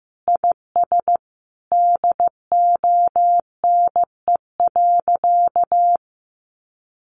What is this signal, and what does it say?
Morse code, heard as a keyed tone, about 15 wpm, 700 Hz. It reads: IS DONE.
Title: Sentences from Top 1500 Words 15wpm